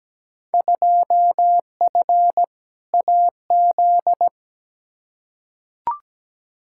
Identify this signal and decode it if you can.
Morse code, heard as a keyed tone, about 17 wpm, 700 Hz. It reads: 2F AZ
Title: Field Day Exchanges - 17wpm